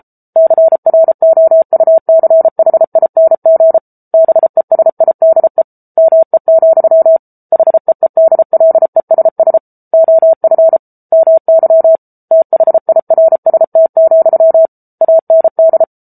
Morse code, heard as a keyed tone, about 33 wpm, 650 Hz. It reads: CROUCHING BESIDE ME, HEEDLESS OF MY THIRST, AND
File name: Book Two / 07-Chapter 7 - 33 WPM